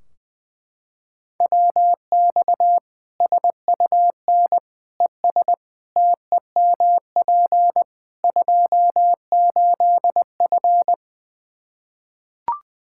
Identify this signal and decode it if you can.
Morse code, heard as a keyed tone, about 20 wpm, 700 Hz. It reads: WX SUN ES TEMP 28F
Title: Practice QSO - Exchange 2 - Info - 20wpm